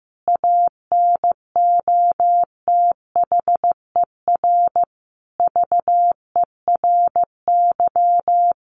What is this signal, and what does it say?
Morse code, heard as a keyed tone, about 15 wpm, 700 Hz. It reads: ANOTHER VERY
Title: Sentences from Top 1100 Words 15wpm